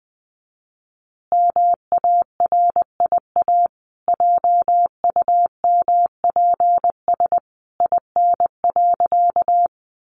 Morse code, heard as a keyed tone, about 20 wpm, 700 Hz. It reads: MARIA JUMPS IN.
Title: Easy Phrases 20wpm